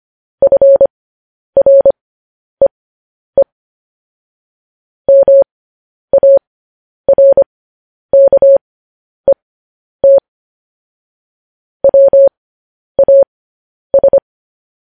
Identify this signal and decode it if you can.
Morse code, heard as a keyed tone, about 25 wpm, spaced as for 10 wpm, 550 Hz. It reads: FREE MARKET WAS
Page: news-headlines-in-morse-code-at-10-wpm